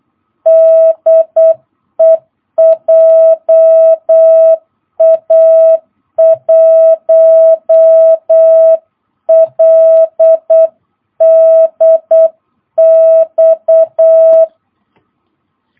Morse code, heard as a keyed tone, about 8 wpm, 660 Hz. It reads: DE JA1LDX